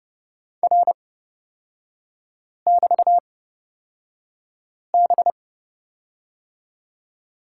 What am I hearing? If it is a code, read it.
Morse code, heard as a keyed tone, about 30 wpm, spaced as for 5 wpm, 700 Hz. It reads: R<BT>B
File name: Lesson 36e - BT - Three Characters 30wpm